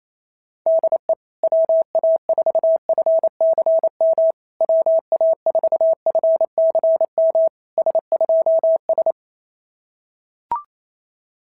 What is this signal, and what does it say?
Morse code, heard as a keyed tone, about 27 wpm, 650 Hz. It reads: DE WA4FCM WA4FCM S2S
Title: SOTA QSO Practice 28wpm